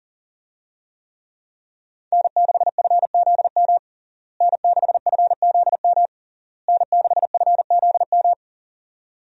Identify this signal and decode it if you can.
Morse code, heard as a keyed tone, about 40 wpm, 700 Hz. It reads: N6FZM N6FZM N6FZM
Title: USA Call Signs - 5 Characters - Repeated 3x - 40wpm